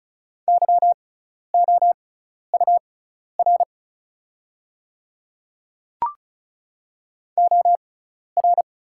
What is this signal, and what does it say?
Morse code, heard as a keyed tone, about 34 wpm, spaced as for 12 wpm, 700 Hz. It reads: YOUR OR
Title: 6x Character Spacing - Top 100 Words - 35wpm